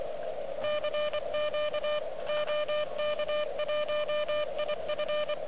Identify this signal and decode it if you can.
Morse code, heard as a keyed tone, about 24 wpm, 600 Hz, with radio noise. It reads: CQ OK1IF
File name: OK QRP 2022 / Zvuk / REC34